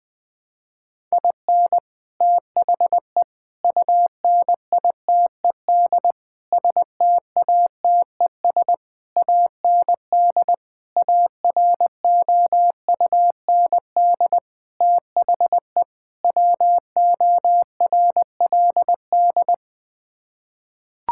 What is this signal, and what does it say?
Morse code, heard as a keyed tone, about 20 wpm, 700 Hz. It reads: IN THE UNITED STATES AND AROUND THE WORLD
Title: Sets of 8 Words - 20wpm